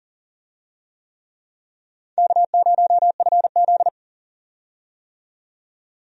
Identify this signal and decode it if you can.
Morse code, heard as a keyed tone, about 40 wpm, 700 Hz. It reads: K0FZ
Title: USA Call Signs - 4 Characters - 40wpm